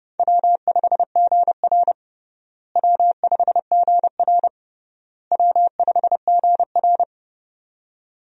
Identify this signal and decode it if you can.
Morse code, heard as a keyed tone, about 32 wpm, 700 Hz. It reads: W5GR W5GR W5GR
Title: USA Call Signs - 4 Characters - Repeated 3x - 30wpm